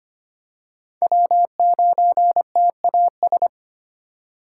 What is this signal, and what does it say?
Morse code, heard as a keyed tone, about 25 wpm, 700 Hz. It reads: W9TAS
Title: USA Call Signs - 5 Characters - 3x Character Spacing - 25wpm